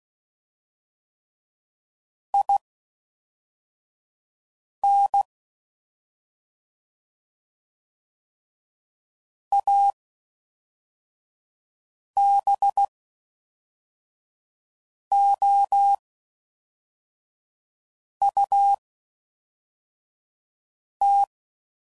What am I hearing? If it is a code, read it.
Morse code, heard as a keyed tone, about 16 wpm, spaced as for 4 wpm, 800 Hz. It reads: IN ABOUT